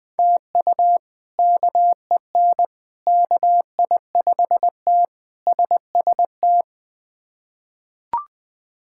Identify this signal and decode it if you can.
Morse code, heard as a keyed tone, about 20 wpm, 700 Hz. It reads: TU KEN KI5T SST
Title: K1USN SST QSO Practice - 20wpm